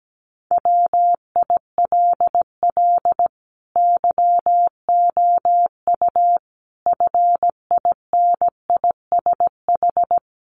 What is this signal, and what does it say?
Morse code, heard as a keyed tone, about 17 wpm, 700 Hz. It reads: WILL YOU FINISH
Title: Sentences from Top 1300 Words 17wpm